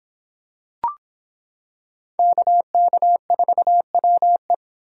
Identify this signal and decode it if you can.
Morse code, heard as a keyed tone, about 26 wpm, 700 Hz. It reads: KK4WE